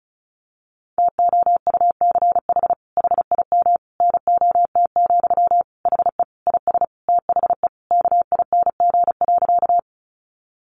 Morse code, heard as a keyed tone, about 34 wpm, 700 Hz. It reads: TOUCH HIM NOT, HE IS THE KING.
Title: Sentences from Top 1200 Words 35wpm